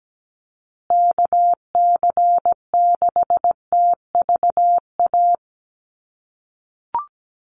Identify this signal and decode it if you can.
Morse code, heard as a keyed tone, about 17 wpm, 700 Hz. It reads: KC6TVA
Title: USA Call Signs Repeated 3x 17wpm